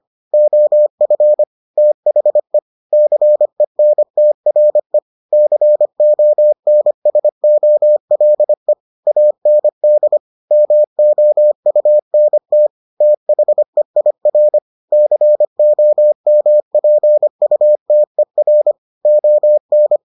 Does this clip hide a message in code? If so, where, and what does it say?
Yes: Morse code from 0.3 to 20.0 s, heard as a keyed tone, about 25 wpm, 600 Hz. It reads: OF THE CENTRE CONSOLE AND MOUNT THEIR COMPUTER ON